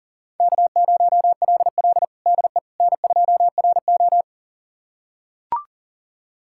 Morse code, heard as a keyed tone, about 40 wpm, 700 Hz. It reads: K0LL DE N2RO
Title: USA Callsign DE Callsign - 4 Characters - 40wpm